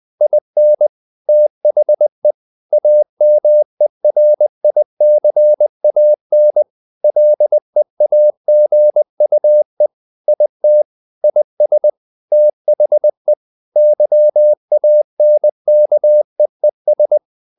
Morse code, heard as a keyed tone, about 20 wpm, 600 Hz. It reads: IN THE AMERICAN LEAGUE IT IS THE YANKEES
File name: the old man and the sea / the old man and the sea part 1 20wpm